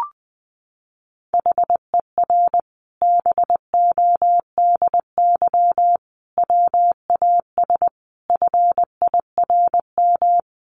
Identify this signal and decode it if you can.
Morse code, heard as a keyed tone, about 19 wpm, 700 Hz. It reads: HER BODY WAS FIRM
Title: Sentences from Top 1400 Words 20wpm